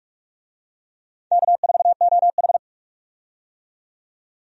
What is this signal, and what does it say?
Morse code, heard as a keyed tone, about 46 wpm, 700 Hz. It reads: K4OH
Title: USA Call Signs - 4 Characters - 45wpm